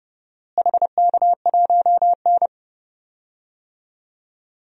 Morse code, heard as a keyed tone, about 30 wpm, 700 Hz. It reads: HK1N
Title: DXCC-100 Call Signs - 4 Characters - Repeated 3x - 30wpm